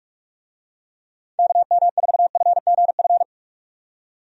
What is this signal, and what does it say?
Morse code, heard as a keyed tone, about 45 wpm, 700 Hz. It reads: KM4FGF
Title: USA Call Signs - 6 Characters - 45wpm